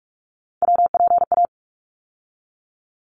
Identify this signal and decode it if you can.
Morse code, heard as a keyed tone, about 46 wpm, 700 Hz. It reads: WPA
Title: ARRL Sections 45wpm